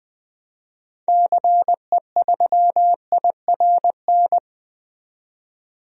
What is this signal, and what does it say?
Morse code, heard as a keyed tone, about 20 wpm, 700 Hz. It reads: CE3IRN
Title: DXCC-100 Call Signs - 6 Characters - 20wpm